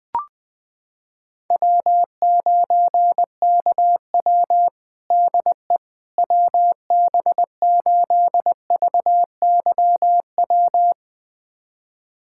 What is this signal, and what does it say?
Morse code, heard as a keyed tone, about 20 wpm, 700 Hz. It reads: W9KW DE WB8VYW
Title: USA Callsign DE Callsign 20wpm